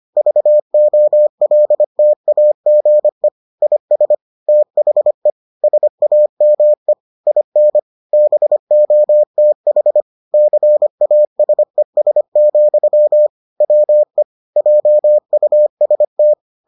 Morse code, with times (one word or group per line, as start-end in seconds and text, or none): 0.2-3.3 VOLTAGE
3.6-4.2 IS
4.5-5.3 THE
5.6-6.9 SAME
7.3-7.8 IN
8.1-10.0 BOTH
10.3-13.3 CASES,
13.6-14.2 WE
14.6-16.3 JUST